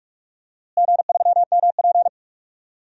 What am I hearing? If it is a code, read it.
Morse code, heard as a keyed tone, about 41 wpm, 700 Hz. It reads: G3MP
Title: DXCC-100 Call Signs - 4 Characters - Repeated 2x - 45wpm